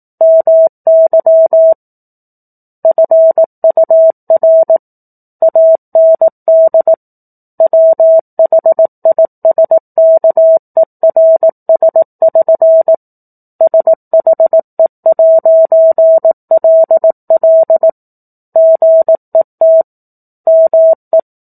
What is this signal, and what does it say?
Morse code, heard as a keyed tone, about 18 wpm, 650 Hz. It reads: MY FUR AND WHISKERS<SN> SHE'LL GET ME
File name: 04-Chapter-04 18 WPM